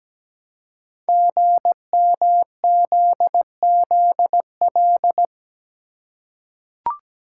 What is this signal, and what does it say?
Morse code, heard as a keyed tone, about 17 wpm, 700 Hz. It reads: GMZZL
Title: Code Groups - 5 Letters - 17wpm